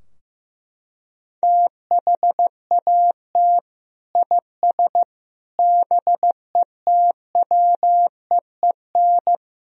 Morse code, heard as a keyed tone, about 16 wpm, 700 Hz. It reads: THAT IS BETWEEN
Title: Sentences from Top 1000 Words - Review Part 4 - 15wpm